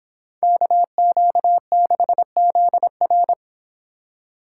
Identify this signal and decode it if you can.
Morse code, heard as a keyed tone, about 26 wpm, 700 Hz. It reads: KQ6ZR
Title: Speed Racing - USA Call Signs 22wpm